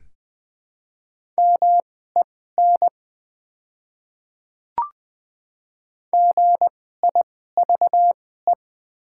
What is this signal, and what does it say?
Morse code, heard as a keyed tone, about 20 wpm, spaced as for 14 wpm, 700 Hz. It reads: MEN GIVE
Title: 2x Character Spacing - Top 200 Words - 20wpm